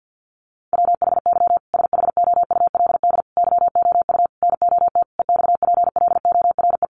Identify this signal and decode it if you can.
Morse code, heard as a keyed tone, about 48 wpm, 700 Hz. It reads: WHY SHOULD YOU NOT EXPLORE
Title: Sentences from Top 1100 Words 50wpm